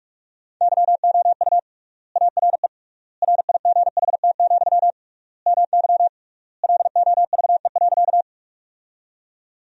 Morse code, heard as a keyed tone, about 46 wpm, 700 Hz. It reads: YOU ARE RIGHT, MY LOVE.
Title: 3x Word-Spacing - Sentences from Top 400 Words - 45wpm